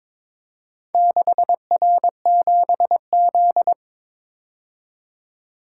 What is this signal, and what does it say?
Morse code, heard as a keyed tone, about 22 wpm, 700 Hz. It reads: 6R7Z
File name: practice / 22wpm / Four Letter-Number Combinations - 22wpm